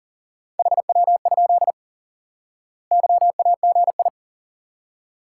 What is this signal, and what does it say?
Morse code, heard as a keyed tone, about 40 wpm, 700 Hz. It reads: HW? YAGI
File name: course / instant-qso-element-course-3 / IQE Course 3 - Lesson 89 - YAGI - Two Elements - 40wpm